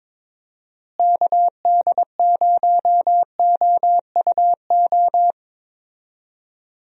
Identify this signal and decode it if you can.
Morse code, heard as a keyed tone, about 22 wpm, 700 Hz. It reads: KD0OUO
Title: USA Call Signs - 6 Characters - Repeated 3x - 22wpm